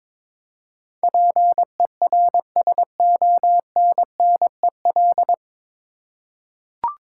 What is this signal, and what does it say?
Morse code, heard as a keyed tone, about 22 wpm, 700 Hz. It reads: PERSONNEL
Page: Morse Code Ninja practice 22wpm